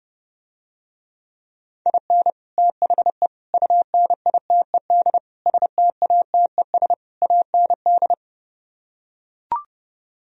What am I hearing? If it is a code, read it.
Morse code, heard as a keyed tone, about 32 wpm, 700 Hz. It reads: IN THE UNITED STATES AND